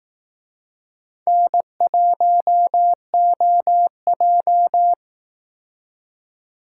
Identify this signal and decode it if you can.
Morse code, heard as a keyed tone, about 18 wpm, 700 Hz. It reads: N1OJ
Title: Speed Racing - USA 1x2 Call Signs 15wpm